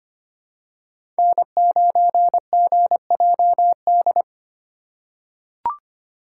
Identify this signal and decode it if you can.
Morse code, heard as a keyed tone, about 25 wpm, 700 Hz. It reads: N9GJD
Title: USA Call Signs - 5 Characters - 2x Character Spacing - 25wpm